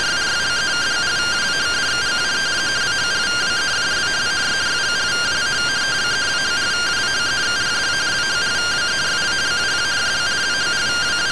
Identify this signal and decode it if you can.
Two-tone teleprinter signal, about 170 Hz shift.